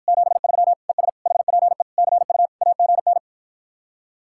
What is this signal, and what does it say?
Morse code, heard as a keyed tone, about 52 wpm, 700 Hz. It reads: 73 ES HPE CU AGN